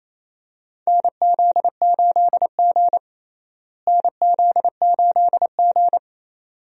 Morse code, heard as a keyed tone, about 28 wpm, 700 Hz. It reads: NZ8G NZ8G
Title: USA Call Signs - 4 Characters - Repeated 2x - 28wpm